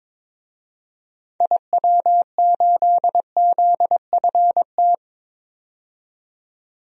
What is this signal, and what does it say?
Morse code, heard as a keyed tone, about 21 wpm, 700 Hz. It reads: IW8ZFT
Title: DXCC-100 Call Signs - 6 Characters - 22wpm